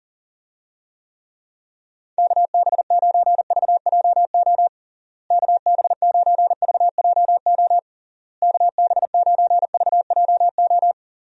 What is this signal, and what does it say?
Morse code, heard as a keyed tone, about 40 wpm, 700 Hz. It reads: KB9VJO KB9VJO KB9VJO
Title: USA Call Signs - 6 Characters - Repeated 3x - 40wpm